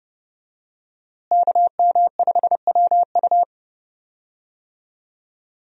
Morse code, heard as a keyed tone, about 30 wpm, 700 Hz. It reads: KM5WU
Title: USA Call Signs - 5 Characters - Repeated 2x - 30wpm